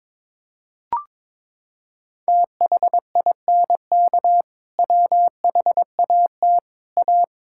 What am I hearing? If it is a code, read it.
Morse code, heard as a keyed tone, about 22 wpm, 700 Hz. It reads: THINK WHAT A
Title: Sentences from Top 1000 Words - Review Part 5 - 22wpm